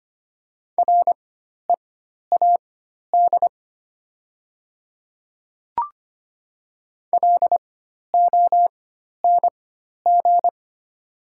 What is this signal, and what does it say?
Morse code, heard as a keyed tone, about 25 wpm, spaced as for 12 wpm, 700 Hz. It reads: READ LONG
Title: 4x Character Spacing - Top 200 Words - 25wpm